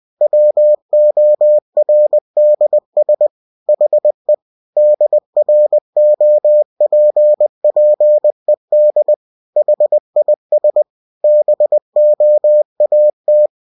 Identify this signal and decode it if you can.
Morse code, heard as a keyed tone, about 20 wpm, 600 Hz. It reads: WORDS HE DROPPED HIS BOAT
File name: cw-books / winnie-the-pooh / winnie-20wpm-page-0095